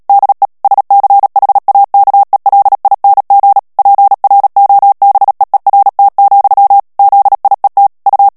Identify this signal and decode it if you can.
Morse code, heard as a keyed tone, about 37 wpm, 800 Hz. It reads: DE SCHAKELING PROBEERT, ZIET U
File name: morse mp3 / HOLLANDAIS texte clair 2 vitesse 35 mots minute